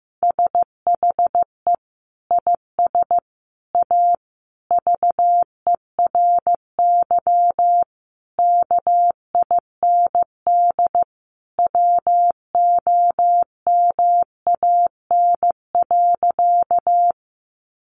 Morse code, heard as a keyed tone, about 15 wpm, 700 Hz. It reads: SHE IS A VERY KIND WOMAN.
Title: Sentences from Top 1100 Words 15wpm